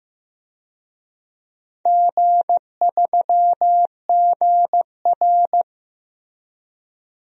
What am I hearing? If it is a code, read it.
Morse code, heard as a keyed tone, about 15 wpm, 700 Hz. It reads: G3GR
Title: DXCC-100 Call Signs - 4 Characters - Repeated 2x - 15wpm